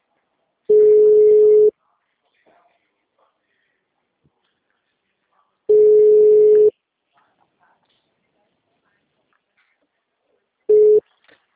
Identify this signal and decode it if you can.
Ringback tone (Europe, 425 Hz), with speech.